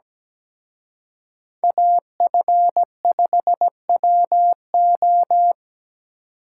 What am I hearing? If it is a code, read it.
Morse code, heard as a keyed tone, about 20 wpm, 700 Hz. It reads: AF5WO
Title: Speed Racing - USA 2x2 Call Signs 17wpm